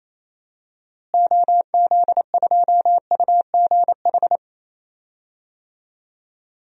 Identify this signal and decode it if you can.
Morse code, heard as a keyed tone, about 28 wpm, 700 Hz. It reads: OZ2UGH
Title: DXCC-100 Call Signs - 6 Characters - Repeated 3x - 28wpm